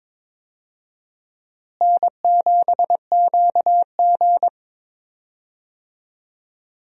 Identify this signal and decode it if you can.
Morse code, heard as a keyed tone, about 22 wpm, 700 Hz. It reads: N7QG
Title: USA Call Signs - 4 Characters - Repeated 2x - 22wpm